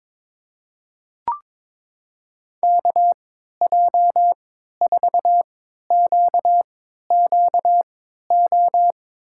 Morse code, heard as a keyed tone, about 22 wpm, spaced as for 13 wpm, 700 Hz. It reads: KJ4QQO
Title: USA Call Signs - 3x Character Spacing - 22wpm